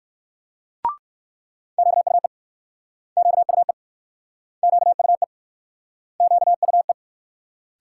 Morse code, heard as a keyed tone, about 60 wpm, 700 Hz. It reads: QUE QUE QUE QUE